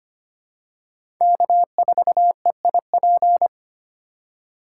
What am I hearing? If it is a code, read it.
Morse code, heard as a keyed tone, about 25 wpm, 700 Hz. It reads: K4EIP
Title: USA Call Signs - 5 Characters - 6x Character Spacing - 25wpm